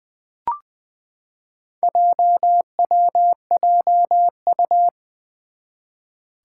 Morse code, heard as a keyed tone, about 20 wpm, 700 Hz. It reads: JWJU
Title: Four Challenging Letter-Number Combinations 20wpm